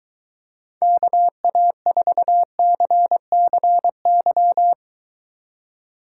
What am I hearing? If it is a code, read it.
Morse code, heard as a keyed tone, about 23 wpm, 700 Hz. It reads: KA4CCY